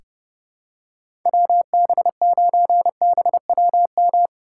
Morse code, heard as a keyed tone, about 31 wpm, 700 Hz. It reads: WB9BWM
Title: USA Call Signs - 6 Characters - 3x Character Spacing - 30wpm